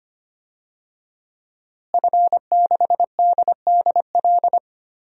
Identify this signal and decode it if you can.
Morse code, heard as a keyed tone, about 25 wpm, 700 Hz. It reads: F6DDL
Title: DXCC-100 Call Signs - 5 Characters - Repeated 3x - 25wpm